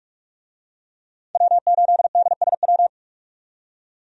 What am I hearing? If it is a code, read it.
Morse code, heard as a keyed tone, about 44 wpm, 700 Hz. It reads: W8DSW
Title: USA Call Signs - 5 Characters - 45wpm